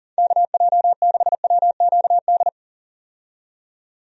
Morse code, heard as a keyed tone, about 40 wpm, 700 Hz. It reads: KJ6WQD
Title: USA Call Signs - 6 Characters - 40wpm